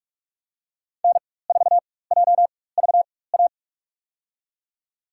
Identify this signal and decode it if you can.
Morse code, heard as a keyed tone, about 46 wpm, spaced as for 21 wpm, 700 Hz. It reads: N4JVA
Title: USA Call Signs - 5 Characters - 4x Character Spacing - 45wpm